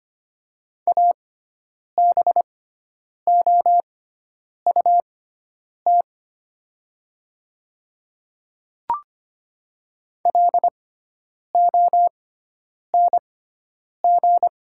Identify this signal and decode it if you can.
Morse code, heard as a keyed tone, about 25 wpm, spaced as for 9 wpm, 700 Hz. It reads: ABOUT LONG
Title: 6x Character Spacing - Top 100 Words - 25wpm